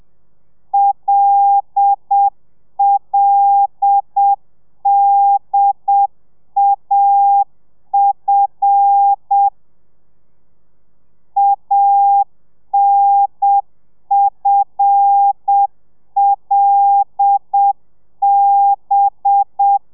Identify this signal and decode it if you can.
Morse code, heard as a keyed tone, about 7 wpm, 800 Hz. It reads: LLDAF ANFLB
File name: telegrafia / i2tfj / Lez5 30